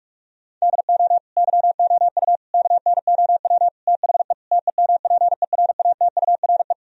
Morse code, heard as a keyed tone, about 46 wpm, 700 Hz. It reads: DO YOU KNOW THE TEMPERATURE
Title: Sentences from Top 1000 Words - Review Part 4 - 45wpm